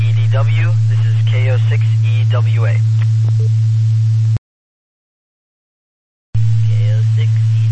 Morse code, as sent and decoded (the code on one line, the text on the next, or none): none